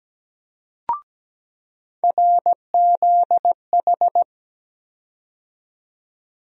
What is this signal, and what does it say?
Morse code, heard as a keyed tone, about 17 wpm, 700 Hz. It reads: RZH
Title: USA Call Sign Postfixes - 3 Letters 17wpm